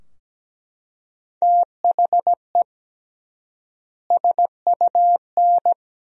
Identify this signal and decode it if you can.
Morse code, heard as a keyed tone, about 17 wpm, 700 Hz. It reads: THE SUN
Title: 3x Word-Spacing - Sentences from Top 300 Words - 17wpm